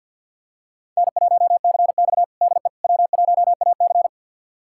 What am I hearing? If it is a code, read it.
Morse code, heard as a keyed tone, about 47 wpm, 700 Hz. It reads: N1CX DE W1AC
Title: USA Callsign DE Callsign - 4 Characters - 50wpm